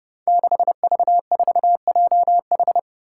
Morse code, heard as a keyed tone, about 31 wpm, 700 Hz. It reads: 6V4JH
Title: Five Challenging Letter-Number Combinations 30wpm